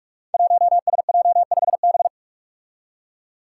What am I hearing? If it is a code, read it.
Morse code, heard as a keyed tone, about 44 wpm, 700 Hz. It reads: 1SJ5B
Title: Five Challenging Letter-Number Combinations 45wpm